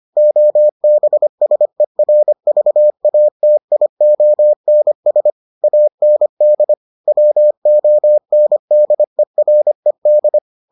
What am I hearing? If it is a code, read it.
Morse code, heard as a keyed tone, about 25 wpm, 600 Hz. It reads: OBSERVATIONS AND WONDERED